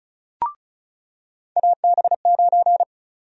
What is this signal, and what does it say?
Morse code, heard as a keyed tone, about 36 wpm, 700 Hz. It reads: AB9
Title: USA Call Sign Prefixes - 35wpm